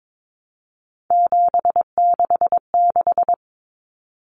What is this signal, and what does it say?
Morse code, heard as a keyed tone, about 22 wpm, 700 Hz. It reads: 766
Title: Common Cut Numbers - 3-Digit 22wpm